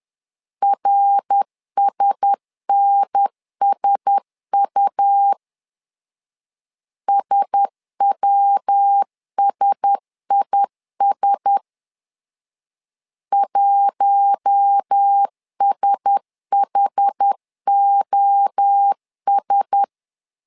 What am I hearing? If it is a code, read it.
Morse code, heard as a keyed tone, about 10 wpm, 800 Hz. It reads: RSNSU SWSIS 1SHOS
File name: LEZIONI / 4 Q1S 10wpm 2133